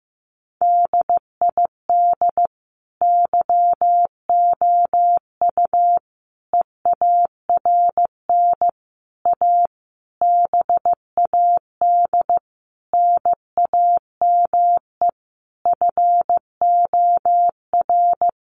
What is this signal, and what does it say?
Morse code, heard as a keyed tone, about 15 wpm, 700 Hz. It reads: DID YOU EARN A BAD NAME FOR
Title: Sentences from Top 1500 Words 15wpm